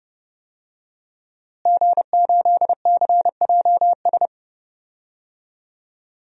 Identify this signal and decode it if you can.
Morse code, heard as a keyed tone, about 30 wpm, 700 Hz. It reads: G8CJS
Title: DXCC-100 Call Signs - 5 Characters - Repeated 3x - 30wpm